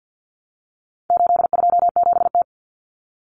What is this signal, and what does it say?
Morse code, heard as a keyed tone, about 50 wpm, 700 Hz. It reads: 827T
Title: Common Cut Numbers - 4-Digit 50wpm